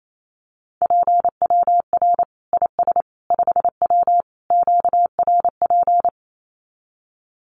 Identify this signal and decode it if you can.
Morse code, heard as a keyed tone, about 28 wpm, 700 Hz. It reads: PWR IS 5W QRP
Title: Common QSO Phrases 28wpm